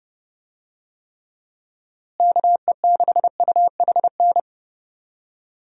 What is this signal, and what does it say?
Morse code, heard as a keyed tone, about 30 wpm, 700 Hz. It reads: KE6UHN
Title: USA Call Signs - 6 Characters - Repeated 3x - 30wpm